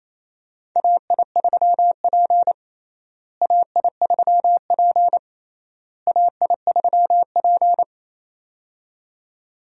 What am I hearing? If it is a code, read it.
Morse code, heard as a keyed tone, about 27 wpm, 700 Hz. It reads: AI3P AI3P AI3P
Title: USA Call Signs - 4 Characters - Repeated 3x - 28wpm